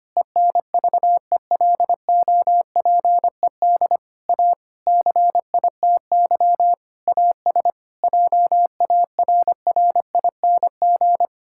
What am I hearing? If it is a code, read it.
Morse code, heard as a keyed tone, about 25 wpm, 700 Hz. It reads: ENVELOPED A CITY AS JARRING
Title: Pangram Challenge v2 - 25wpm